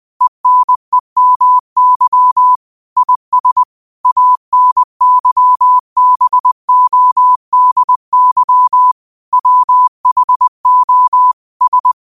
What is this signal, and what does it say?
Morse code, heard as a keyed tone, about 20 wpm, 1000 Hz. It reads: ENEMY IS ANYBODY WHO S
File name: qotdfiles / 20wpm